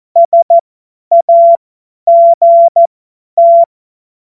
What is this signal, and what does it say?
Morse code, heard as a keyed tone, about 14 wpm, spaced as for 10 wpm, 680 Hz. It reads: SAGT
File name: Morsekurs / Witze / Witz 28. September 2021, 50 F70